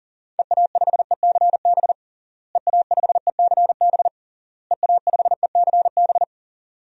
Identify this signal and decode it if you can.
Morse code, heard as a keyed tone, about 40 wpm, 700 Hz. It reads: EA5ECB EA5ECB EA5ECB
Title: DXCC-100 Call Signs - 6 Characters - Repeated 3x - 40wpm